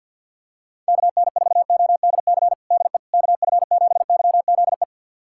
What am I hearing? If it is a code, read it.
Morse code, heard as a keyed tone, about 46 wpm, 700 Hz. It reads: KN4ODC DE KF7YZE
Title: USA Callsign DE Callsign 50wpm